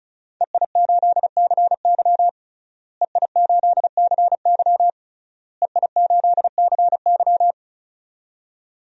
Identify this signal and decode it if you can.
Morse code, heard as a keyed tone, about 36 wpm, 700 Hz. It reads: EI8CY EI8CY EI8CY
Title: DXCC-100 Call Signs - 5 Characters - Repeated 3x - 35wpm